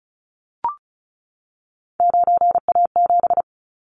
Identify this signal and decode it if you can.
Morse code, heard as a keyed tone, about 34 wpm, 700 Hz. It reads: 9A7
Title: Callsign Prefixes for Top 50 DXCC - 35wpm